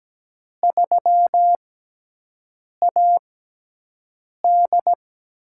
Morse code, heard as a keyed tone, about 17 wpm, spaced as for 6 wpm, 700 Hz. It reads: 3AD
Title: USA Call Signs - 4 Characters - 6x Character Spacing - 17wpm